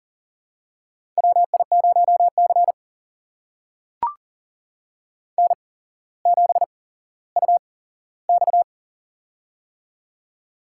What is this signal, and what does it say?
Morse code, heard as a keyed tone, about 40 wpm, spaced as for 17 wpm, 700 Hz. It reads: WI0C N7UX
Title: USA Call Signs - 4 Characters - 8x Character Spacing - 40wpm